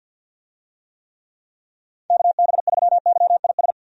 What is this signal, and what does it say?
Morse code, heard as a keyed tone, about 50 wpm, 700 Hz. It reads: KB3YIS
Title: USA Call Signs - 6 Characters - 50wpm